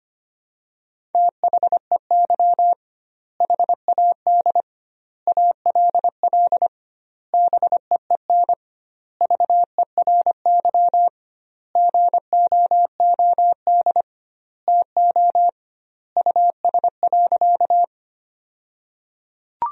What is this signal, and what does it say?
Morse code, heard as a keyed tone, about 25 wpm, 700 Hz. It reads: THEY HAD ALL BEEN VERY GOOD TO US.
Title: 2x Word-Spacing - Sentences from Top 200 Words - 25wpm